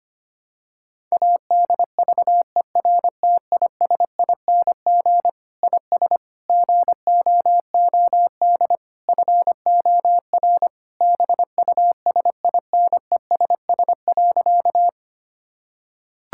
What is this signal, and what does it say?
Morse code, heard as a keyed tone, about 25 wpm, 700 Hz. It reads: ADVERTISING IS GOOD FOR BUSINESS.
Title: Sentences from Top 1000 Words - Review Part 6 - 25wpm